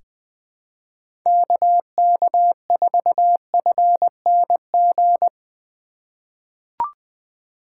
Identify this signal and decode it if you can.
Morse code, heard as a keyed tone, about 20 wpm, 700 Hz. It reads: KK4FNG
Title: USA Call Signs - 4x Character Spacing - 20wpm